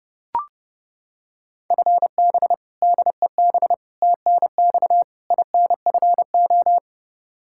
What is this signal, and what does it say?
Morse code, heard as a keyed tone, about 30 wpm, 700 Hz. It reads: FB DEB TNX INFO
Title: Practice QSO - Exchange 3 - More Info - 30wpm